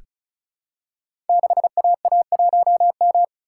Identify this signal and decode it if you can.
Morse code, heard as a keyed tone, about 36 wpm, 700 Hz. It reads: 6AA1M
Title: Code Groups - 5 Letter:Numbers - 35wpm